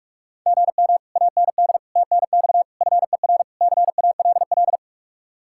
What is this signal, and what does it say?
Morse code, heard as a keyed tone, about 45 wpm, 700 Hz. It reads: GM AND TNX FER CALL